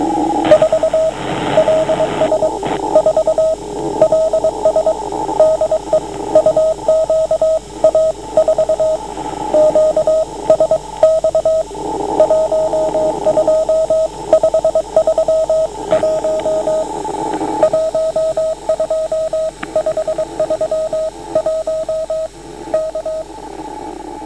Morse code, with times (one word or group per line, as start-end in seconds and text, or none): 0.5-1.1 4
1.6-2.5 LS
3.0-3.5 4
4.0-4.9 LS
5.4-6.0 DE
6.4-9.0 UQA4
9.5-11.6 QSX
12.2-16.8 12531
17.6-22.3 12531
22.7-23.2 K